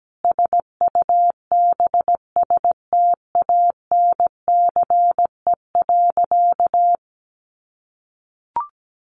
Morse code, heard as a keyed tone, about 17 wpm, 700 Hz. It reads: SUBSTANCE.
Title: Sentences from Top 1500 Words 17wpm